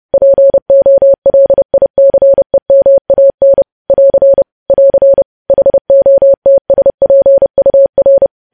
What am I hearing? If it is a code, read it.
Morse code, heard as a keyed tone, about 30 wpm, 550 Hz. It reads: POLICEMAN <AR> <AR> HOTSPUR
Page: news headlines in morse code at 30 wpm